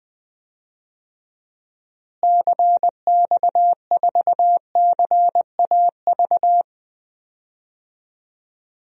Morse code, heard as a keyed tone, about 21 wpm, 700 Hz. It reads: CX4CAV
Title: DXCC-100 Call Signs - 6 Characters - Repeated 3x - 20wpm